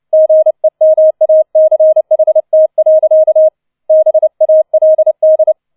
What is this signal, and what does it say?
Morse code, heard as a keyed tone, about 29 wpm, 630 Hz. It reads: GEMACHT. BALD